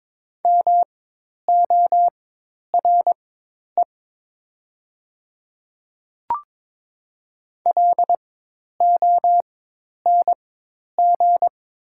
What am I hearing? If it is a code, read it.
Morse code, heard as a keyed tone, about 22 wpm, spaced as for 10 wpm, 700 Hz. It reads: MORE LONG
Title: 4x Character Spacing - Top 100 Words - 22wpm